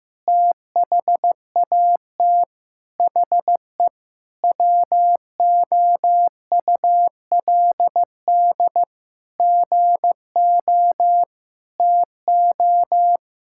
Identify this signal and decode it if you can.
Morse code, heard as a keyed tone, about 15 wpm, 700 Hz. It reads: THAT HE WOULD GO TO
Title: Sentences from Top 1600 Words 15wpm